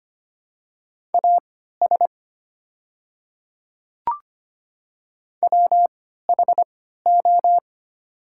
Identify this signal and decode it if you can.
Morse code, heard as a keyed tone, about 25 wpm, spaced as for 14 wpm, 700 Hz. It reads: AS WHO
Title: 3x Character Spacing - Top 200 Words - 25wpm